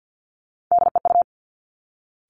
Morse code, heard as a keyed tone, about 50 wpm, 700 Hz. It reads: DEU
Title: Three Challenging Letter-Number Combinations 50wpm